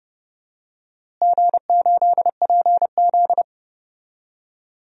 Morse code, heard as a keyed tone, about 30 wpm, 700 Hz. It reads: G8PZ
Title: DXCC-100 Call Signs - 4 Characters - Repeated 2x - 30wpm